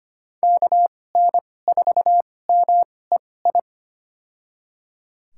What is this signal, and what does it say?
Morse code, heard as a keyed tone, about 25 wpm, spaced as for 18 wpm, 700 Hz. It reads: KN4MEI
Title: USA Call Signs - 2x Character Spacing - 25wpm